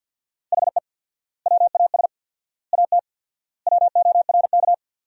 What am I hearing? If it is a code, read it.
Morse code, heard as a keyed tone, about 51 wpm, 700 Hz. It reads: HE WAS AT WORK
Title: 4x Word-Spacing - Sentences from Top 200 Words - 50wpm